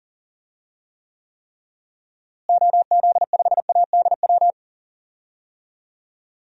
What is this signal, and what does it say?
Morse code, heard as a keyed tone, about 40 wpm, 700 Hz. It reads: OZ5ADW
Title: DXCC-100 Call Signs - 6 Characters - Repeated 3x - 40wpm